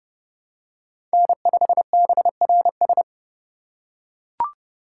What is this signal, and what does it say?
Morse code, heard as a keyed tone, about 31 wpm, 700 Hz. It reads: N5BRS
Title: USA Call Signs - 6x Character Spacing - 30wpm